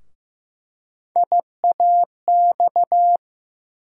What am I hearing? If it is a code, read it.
Morse code, heard as a keyed tone, about 15 wpm, 700 Hz. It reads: IAX
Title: USA Call Sign Postfixes - 3 Letters 15wpm